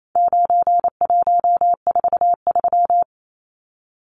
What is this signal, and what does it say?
Morse code, heard as a keyed tone, about 28 wpm, 700 Hz. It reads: 9143
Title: Numbers - 4-Digit 28wpm